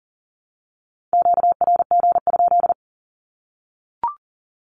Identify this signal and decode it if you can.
Morse code, heard as a keyed tone, about 40 wpm, 700 Hz. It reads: QRG?
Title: Q-Codes 40wpm